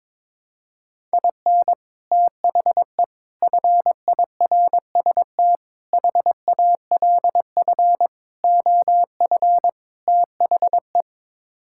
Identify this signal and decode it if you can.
Morse code, heard as a keyed tone, about 22 wpm, 700 Hz. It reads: IN THE FIRST HALF OF THE
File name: practice / 22wpm / Sets of 6 Words - 22wpm